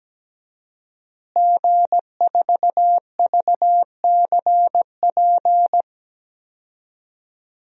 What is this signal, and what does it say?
Morse code, heard as a keyed tone, about 17 wpm, 700 Hz. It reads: G4VCP
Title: DXCC-100 Call Signs - 5 Characters - Repeated 2x - 17wpm